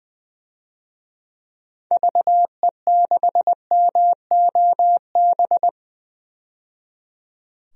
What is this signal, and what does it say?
Morse code, heard as a keyed tone, about 20 wpm, 700 Hz. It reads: VE6MOB
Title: DXCC-100 Call Signs - 6 Characters - 20wpm